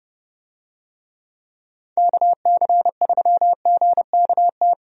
Morse code, heard as a keyed tone, about 30 wpm, 700 Hz. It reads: KC3GKT
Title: USA Call Signs - 6 Characters - Repeated 3x - 30wpm